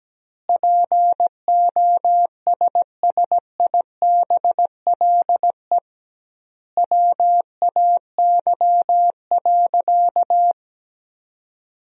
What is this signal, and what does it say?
Morse code, heard as a keyed tone, about 17 wpm, 700 Hz. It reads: POSSIBLE WAY.
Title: 2x Word-Spacing - Sentences from Top 500 Words - 17wpm